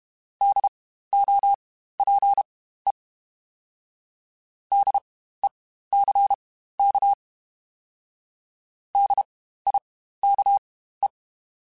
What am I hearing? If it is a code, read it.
Morse code, heard as a keyed tone, about 32 wpm, spaced as for 15 wpm, 800 Hz. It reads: DOPE DECK DIKE
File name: Spaced Alphas / Spaced Alphas mp3s / Ds sp @30